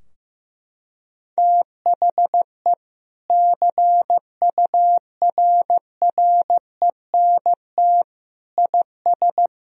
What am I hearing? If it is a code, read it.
Morse code, heard as a keyed tone, about 15 wpm, 700 Hz. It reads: THE CURRENT IS
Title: Sentences from Top 1000 Words - Review Part 4 - 15wpm